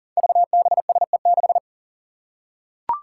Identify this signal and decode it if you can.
Morse code, heard as a keyed tone, about 40 wpm, 700 Hz. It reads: VBSE6
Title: Five Challenging Letter-Number Combinations 40wpm